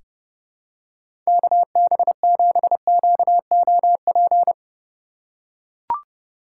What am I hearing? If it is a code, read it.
Morse code, heard as a keyed tone, about 30 wpm, 700 Hz. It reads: KB7QOP
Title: K1USN SST QSO Practice - 30wpm